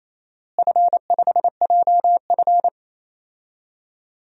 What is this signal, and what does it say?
Morse code, heard as a keyed tone, about 28 wpm, 700 Hz. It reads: F5JF
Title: DXCC-100 Call Signs - 4 Characters - Repeated 3x - 28wpm